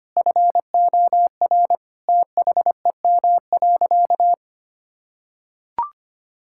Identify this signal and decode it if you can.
Morse code, heard as a keyed tone, about 25 wpm, 700 Hz. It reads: FOR THEM.
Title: Sentences from Top 1000 Words - Review Part 5 - 25wpm